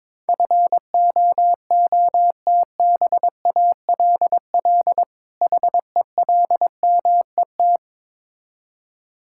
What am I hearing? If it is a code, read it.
Morse code, heard as a keyed tone, about 22 wpm, 700 Hz. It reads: FOOTBALL HELMET
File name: Difficult Words with Hints - 22wpm